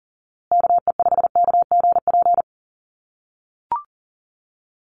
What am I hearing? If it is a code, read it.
Morse code, heard as a keyed tone, about 40 wpm, 700 Hz. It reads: KE5KGP
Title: USA Call Signs Repeated 3x 40wpm